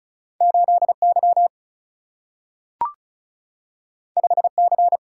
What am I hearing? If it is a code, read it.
Morse code, heard as a keyed tone, about 36 wpm, 700 Hz. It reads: 8 Y 5 C
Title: Single-Letter Single-Number 35wpm